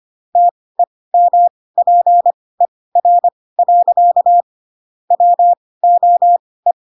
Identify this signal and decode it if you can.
Morse code, heard as a keyed tone, about 25 wpm, spaced as for 18 wpm, 700 Hz. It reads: TEMPER. WOE
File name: Pinocchio / Pinocchio ch2 12 25 ik1hge